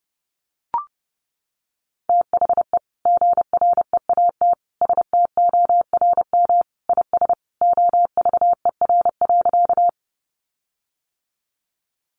Morse code, heard as a keyed tone, about 30 wpm, 700 Hz. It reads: THE GREAT STORM IS OVER.
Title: Sentences from Top 1500 Words 30wpm